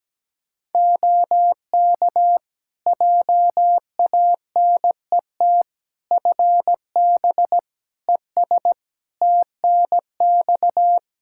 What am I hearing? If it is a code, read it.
Morse code, heard as a keyed tone, about 17 wpm, 700 Hz. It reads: OK JANET FB ES TNX
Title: Practice QSO - Exchange 2 - Info - 17wpm